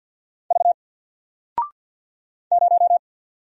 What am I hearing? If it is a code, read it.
Morse code, heard as a keyed tone, about 49 wpm, 700 Hz. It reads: V 0
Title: Single Character (Letter:Number:Punctuation:Prosign) 50wpm